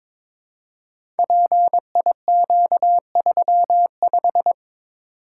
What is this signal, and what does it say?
Morse code, heard as a keyed tone, about 22 wpm, 700 Hz. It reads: PIQ35
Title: Code Groups - 5 Letter:Numbers - 22wpm